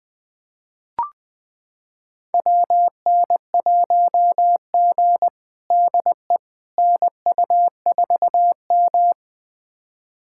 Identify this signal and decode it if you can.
Morse code, heard as a keyed tone, about 20 wpm, 700 Hz. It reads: WN1G DE NU4M
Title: USA Callsign DE Callsign - 4 Characters - 20wpm